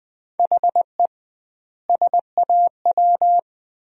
Morse code, heard as a keyed tone, about 21 wpm, 700 Hz. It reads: HE SAW
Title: 2x Word-Spacing - Sentences from Top 500 Words - 20wpm